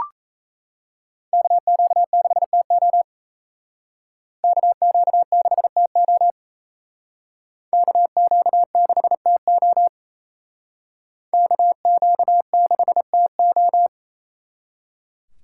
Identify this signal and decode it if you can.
Morse code, heard as a keyed tone, about 35 wpm, 700 Hz. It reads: KQ6TO KQ6TO KQ6TO KQ6TO
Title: Speed Racing - USA 2x2 Call Signs 28wpm